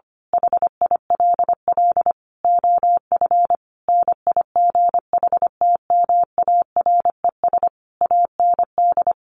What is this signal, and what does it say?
Morse code, heard as a keyed tone, about 25 wpm, 700 Hz. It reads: HILL OF NIGHTMARES AND
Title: A Sound Of Thunder 25wpm